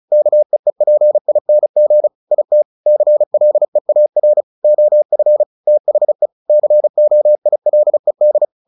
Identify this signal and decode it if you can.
Morse code, heard as a keyed tone, about 34 wpm, 600 Hz. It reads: KEEPING IT CLEAR OF THE COILED
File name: the old man and the sea / the old man and the sea 1 35wpm